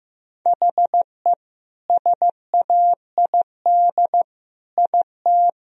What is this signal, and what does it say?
Morse code, heard as a keyed tone, about 15 wpm, 700 Hz. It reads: HE SAID IT
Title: Sentences from Top 1000 Words - Review Part 2 - 15wpm